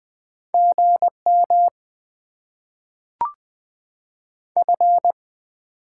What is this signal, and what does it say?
Morse code, heard as a keyed tone, about 21 wpm, 700 Hz. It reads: GM F
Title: Top 20 QSO Elements - 8x Character Spacing - 20wpm